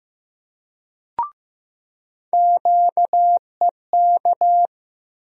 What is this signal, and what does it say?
Morse code, heard as a keyed tone, about 15 wpm, 700 Hz. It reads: QEK
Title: USA Call Sign Postfixes - 3 Letters 15wpm